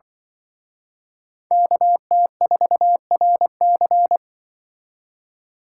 Morse code, heard as a keyed tone, about 25 wpm, 700 Hz. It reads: KT4RC
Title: Speed Racing - USA Call Signs 20wpm